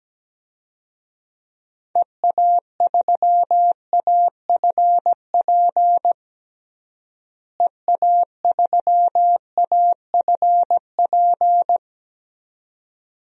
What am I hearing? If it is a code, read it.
Morse code, heard as a keyed tone, about 17 wpm, 700 Hz. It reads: EA3AFP EA3AFP
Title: DXCC-100 Call Signs - 6 Characters - Repeated 2x - 17wpm